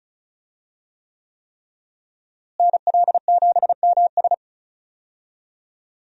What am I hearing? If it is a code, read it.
Morse code, heard as a keyed tone, about 34 wpm, 700 Hz. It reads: NL7MS
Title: USA Call Signs - 5 Characters - Repeated 3x - 35wpm